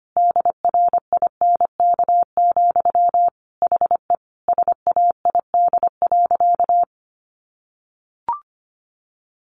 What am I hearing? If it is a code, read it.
Morse code, heard as a keyed tone, about 25 wpm, 700 Hz. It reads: DRINK, HE SAID.
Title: Sentences from Top 1500 Words 25wpm